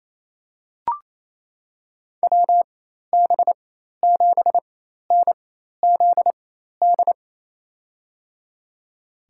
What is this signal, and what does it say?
Morse code, heard as a keyed tone, about 28 wpm, spaced as for 13 wpm, 700 Hz. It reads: WB7NZD
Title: USA Call Signs - 6 Characters - 4x Character Spacing - 28wpm